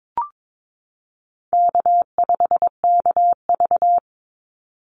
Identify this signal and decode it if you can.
Morse code, heard as a keyed tone, about 22 wpm, 700 Hz. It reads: K5KV
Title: Call Signs Repeated 2x - 22wpm